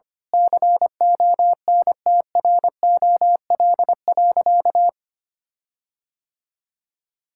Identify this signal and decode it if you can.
Morse code, heard as a keyed tone, about 25 wpm, 700 Hz. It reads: CONTROL.
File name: practice / 25wpm / Sentences with 9-Word Phrases - 25wpm